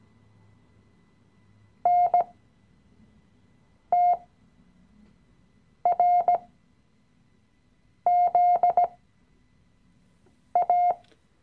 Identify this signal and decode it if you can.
Morse code, heard as a keyed tone, about 17 wpm, spaced as for 5 wpm, 710 Hz. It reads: NTRZA
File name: CW-Online-Kurs / Anfänger / Lektion 018